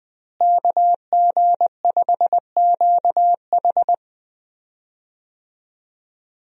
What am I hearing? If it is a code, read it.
Morse code, heard as a keyed tone, about 20 wpm, 700 Hz. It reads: KG5QH
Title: USA Call Signs - 5 Characters - Repeated 3x - 20wpm